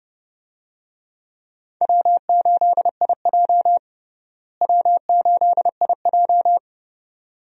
Morse code, heard as a keyed tone, about 32 wpm, 700 Hz. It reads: W8IJ W8IJ
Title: USA Call Signs - 4 Characters - Repeated 2x - 30wpm